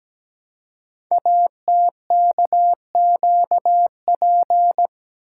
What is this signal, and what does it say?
Morse code, heard as a keyed tone, about 17 wpm, 700 Hz. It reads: ATKQP
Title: Code Groups - 5 Letters - 17wpm